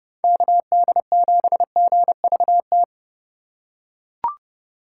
Morse code, heard as a keyed tone, about 30 wpm, 700 Hz. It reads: KD7GVT
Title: USA Call Signs - 4x Character Spacing - 30wpm